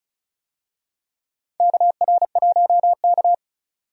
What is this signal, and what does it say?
Morse code, heard as a keyed tone, about 34 wpm, 700 Hz. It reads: KR1K
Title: USA Call Signs - 4 Characters - 35wpm